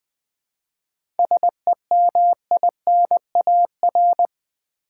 Morse code, heard as a keyed tone, about 20 wpm, 700 Hz. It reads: SEMINAR
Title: 7 Letter Words - First 3 Letters Given - 20wpm